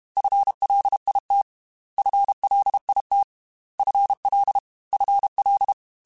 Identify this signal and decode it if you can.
Morse code, heard as a keyed tone, about 32 wpm, 800 Hz. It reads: FLIT FLIT FL FL
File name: FL Prefixes 3x2 @30